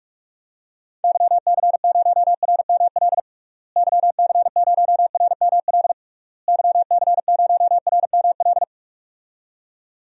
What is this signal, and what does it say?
Morse code, heard as a keyed tone, about 46 wpm, 700 Hz. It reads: YC0RML YC0RML YC0RML
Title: DXCC-100 Call Signs - 6 Characters - Repeated 3x - 45wpm